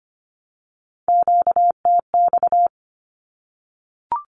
Morse code, heard as a keyed tone, about 25 wpm, 700 Hz. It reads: QTX
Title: Q-Codes 25wpm